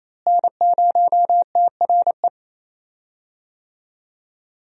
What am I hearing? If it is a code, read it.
Morse code, heard as a keyed tone, about 28 wpm, 700 Hz. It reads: N0TRE
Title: USA Call Signs - 5 Characters - Repeated 2x - 28wpm